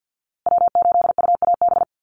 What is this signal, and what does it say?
Morse code, heard as a keyed tone, about 50 wpm, 700 Hz. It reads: W8UAB
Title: USA Call Signs Repeated 2x 50wpm